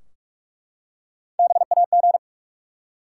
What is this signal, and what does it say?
Morse code, heard as a keyed tone, about 44 wpm, 700 Hz. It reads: BAG
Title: Three Letter Words - 45wpm